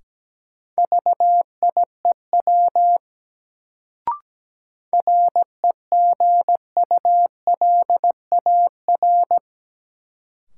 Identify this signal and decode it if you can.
Morse code, heard as a keyed tone, about 17 wpm, 700 Hz. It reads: VIEW REGULAR
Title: Sentences from Top 900 Words - New Word Drill - 17wpm